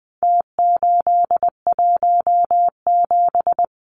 Morse code, heard as a keyed tone, about 20 wpm, 700 Hz. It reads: T817
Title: Numbers - Contest Style - 4-Digit 20wpm